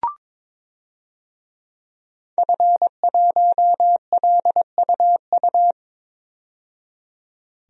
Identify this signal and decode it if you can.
Morse code, heard as a keyed tone, about 22 wpm, 700 Hz. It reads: F1LUU
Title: DXCC-100 Call Signs - 5 Characters - 22wpm